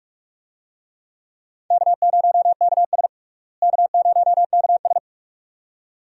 Morse code, heard as a keyed tone, about 44 wpm, 700 Hz. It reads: K0KS K0KS
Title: USA Call Signs - 4 Characters - Repeated 2x - 45wpm